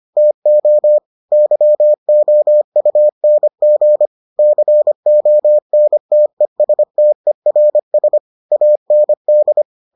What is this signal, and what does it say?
Morse code, heard as a keyed tone, about 25 wpm, 600 Hz. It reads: TO YOUNG CONTESTERS AND